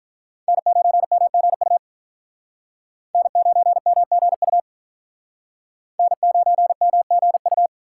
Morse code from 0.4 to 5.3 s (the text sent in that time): N9MGU N9MGU